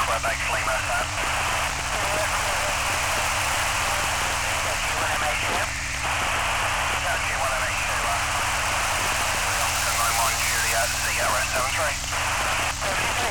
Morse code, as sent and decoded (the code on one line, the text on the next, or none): none